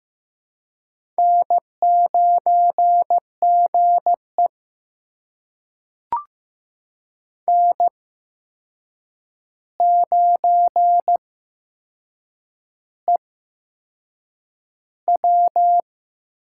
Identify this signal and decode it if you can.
Morse code, heard as a keyed tone, about 15 wpm, spaced as for 6 wpm, 700 Hz. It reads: N9GE N9EW
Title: USA Call Signs - 4 Characters - 8x Character Spacing - 15wpm